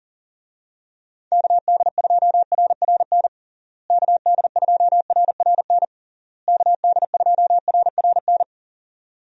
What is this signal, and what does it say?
Morse code, heard as a keyed tone, about 37 wpm, 700 Hz. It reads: KD2RRN KD2RRN KD2RRN
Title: USA Call Signs - 6 Characters - Repeated 3x - 40wpm